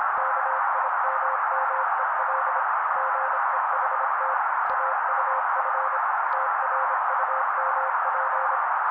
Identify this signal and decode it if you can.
Morse code, heard as a keyed tone, about 26 wpm, 530 Hz. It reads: KIMMEL GEHT AUF TRUMP
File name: daily / 25wpm noise